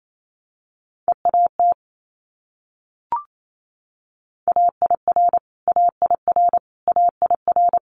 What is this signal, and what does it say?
Morse code, heard as a keyed tone, about 28 wpm, 700 Hz. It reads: EAT AIR AIR AIR
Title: Multi-letter Phonemes Repeated 3x 28wpm